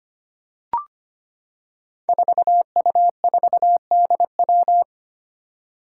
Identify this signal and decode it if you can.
Morse code, heard as a keyed tone, about 25 wpm, 700 Hz. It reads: 4U4DW
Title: Five Challenging Letter-Number Combinations 25wpm